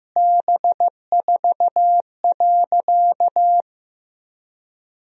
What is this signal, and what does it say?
Morse code, heard as a keyed tone, about 16 wpm, 700 Hz. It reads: B4.
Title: Sentences from Top 400 Words and Abbreviations 15wpm